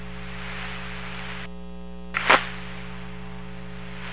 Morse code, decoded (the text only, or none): none